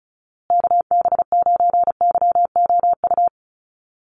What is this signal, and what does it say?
Morse code, heard as a keyed tone, about 34 wpm, 700 Hz. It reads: KB9YOU
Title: USA Call Signs Repeated 2x 35wpm